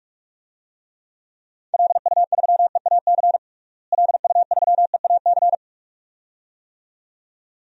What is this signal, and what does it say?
Morse code, heard as a keyed tone, about 41 wpm, 700 Hz. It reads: LU3EAC LU3EAC
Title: DXCC-100 Call Signs - 6 Characters - Repeated 2x - 45wpm